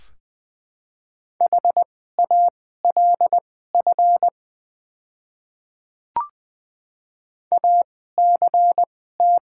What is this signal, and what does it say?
Morse code, heard as a keyed tone, about 20 wpm, spaced as for 14 wpm, 700 Hz. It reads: HALF ACT
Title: 2x Character Spacing - Top 500 Words - 20wpm